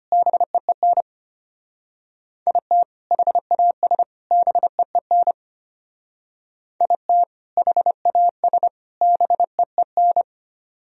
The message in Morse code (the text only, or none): BEEN IT HAS BEEN IT HAS BEEN